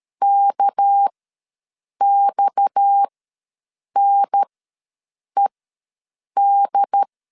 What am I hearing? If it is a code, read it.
Morse code, heard as a keyed tone, about 13 wpm, spaced as for 7 wpm, 800 Hz. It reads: KXNED
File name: LEZIONI / 6 PKX0 / 6 PKX0 ws 12wpm 3448